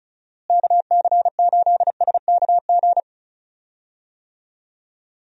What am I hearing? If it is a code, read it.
Morse code, heard as a keyed tone, about 36 wpm, 700 Hz. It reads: KC8SKG
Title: USA Call Signs - 6 Characters - Repeated 2x - 35wpm